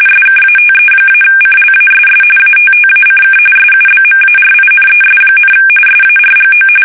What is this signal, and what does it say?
RTTY, about 45.45 baud. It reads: SKATE-CLASS ATTACK SUBMARINES, SARGO
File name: fsk850 / usnavy5